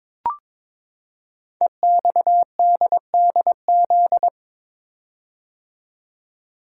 Morse code, heard as a keyed tone, about 22 wpm, 700 Hz. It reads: EXDDZ
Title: Code Groups - 5 Letters - 22wpm